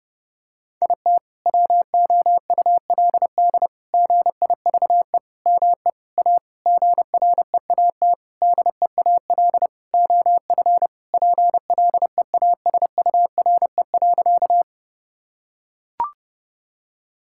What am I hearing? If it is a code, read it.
Morse code, heard as a keyed tone, about 30 wpm, 700 Hz. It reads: IT WOULD GIVE ME A GREAT DEAL OF PLEASURE.
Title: Sentences from Top 1600 Words 30wpm